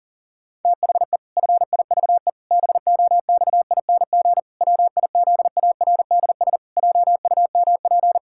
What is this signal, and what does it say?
Morse code, heard as a keyed tone, about 40 wpm, 700 Hz. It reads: THE FIVE BOXING WIZARDS JUMP
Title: Pangram Challenge v2 - 40wpm